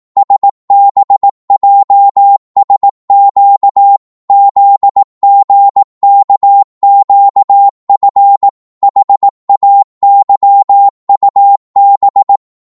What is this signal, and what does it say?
Morse code, heard as a keyed tone, about 18 wpm, 800 Hz. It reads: SBJSQ ZGKQF HAYUB
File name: Morse / ltr-18wpm